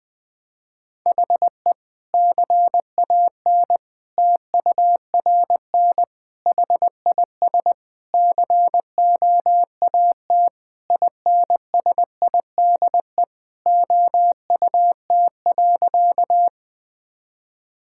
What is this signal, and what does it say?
Morse code, heard as a keyed tone, about 21 wpm, 700 Hz. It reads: HE CAN TURN HIS COAT INSIDE OUT.
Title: Sentences from Top 1600 Words 20wpm